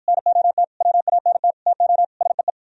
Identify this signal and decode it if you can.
Morse code, heard as a keyed tone, about 52 wpm, 700 Hz. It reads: NOT WANT TO SEE